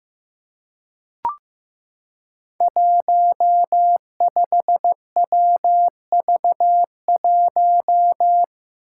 Morse code, heard as a keyed tone, about 15 wpm, 700 Hz. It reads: 15WV1
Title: Five Challenging Letter-Number Combinations 15wpm